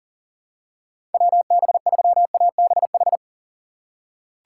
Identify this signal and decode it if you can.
Morse code, heard as a keyed tone, about 40 wpm, 700 Hz. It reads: WB3ABH
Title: USA Call Signs - 6 Characters - 2x Character Spacing - 40wpm